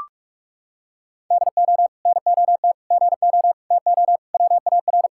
Morse code, heard as a keyed tone, about 41 wpm, 700 Hz. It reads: DO NOT GO TO WAR
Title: Sentences from Top 1000 Words - Review Part 5 - 45wpm